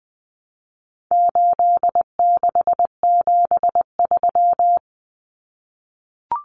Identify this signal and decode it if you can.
Morse code, heard as a keyed tone, about 19 wpm, 700 Hz. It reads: 8673
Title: Common Cut Numbers - 4-Digit 20wpm